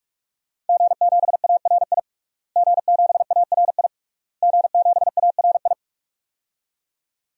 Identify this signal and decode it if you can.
Morse code, heard as a keyed tone, about 48 wpm, 700 Hz. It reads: G7ARI G7ARI G7ARI
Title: DXCC-100 Call Signs - 5 Characters - Repeated 3x - 45wpm